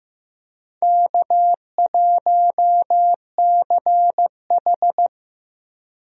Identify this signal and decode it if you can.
Morse code, heard as a keyed tone, about 15 wpm, 700 Hz. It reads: K1CH
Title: USA Call Signs - 4 Characters - 15wpm